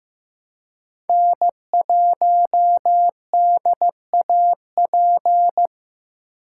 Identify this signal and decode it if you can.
Morse code, heard as a keyed tone, about 15 wpm, 700 Hz. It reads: N1DAP
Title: USA Call Signs - 5 Characters - 2x Character Spacing - 15wpm